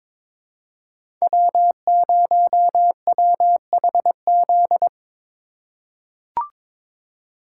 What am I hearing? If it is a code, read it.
Morse code, heard as a keyed tone, about 22 wpm, 700 Hz. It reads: W0WHZ
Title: USA Call Signs - 5 Characters - 4x Character Spacing - 22wpm